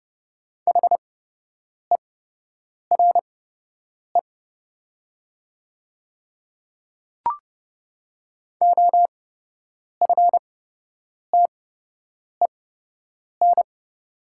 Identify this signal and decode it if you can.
Morse code, heard as a keyed tone, about 30 wpm, spaced as for 8 wpm, 700 Hz. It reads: HERE OFTEN
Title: 8x Character Spacing - Top 300 Words - 30wpm